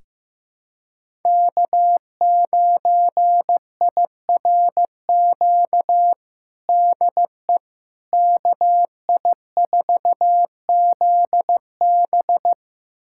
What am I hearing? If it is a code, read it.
Morse code, heard as a keyed tone, about 15 wpm, 700 Hz. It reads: K9IRQ DE KI4ZB
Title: USA Callsign DE Callsign - 5 Characters - 15wpm